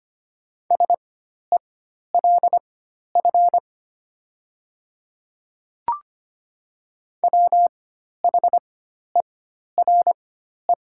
Morse code, heard as a keyed tone, about 25 wpm, spaced as for 12 wpm, 700 Hz. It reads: SELF WHERE
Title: 4x Character Spacing - Top 300 Words - 25wpm